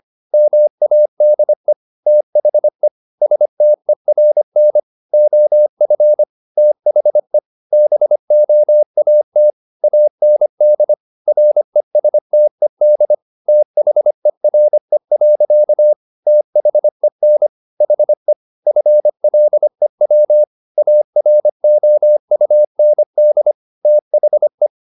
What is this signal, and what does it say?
Morse code, heard as a keyed tone, about 25 wpm, 600 Hz. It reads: MADE THE STERN OF THE BOAT AND RESTED THERE. THEN HE FLEW AROUND THE
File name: the old man and the sea part 1 25wpm